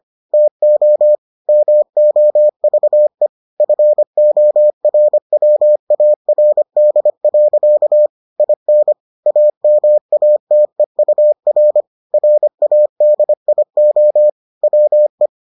Morse code, heard as a keyed tone, about 25 wpm, 600 Hz. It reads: TO MOVE FORWARD. IN AMATEUR RADIO WE